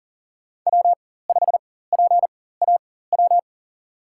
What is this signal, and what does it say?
Morse code, heard as a keyed tone, about 40 wpm, spaced as for 19 wpm, 700 Hz. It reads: W5PAW
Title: USA Call Signs - 5 Characters - 4x Character Spacing - 40wpm